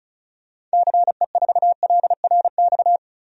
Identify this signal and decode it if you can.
Morse code, heard as a keyed tone, about 34 wpm, 700 Hz. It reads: CE4LRX
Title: DXCC-100 Call Signs - 6 Characters - 35wpm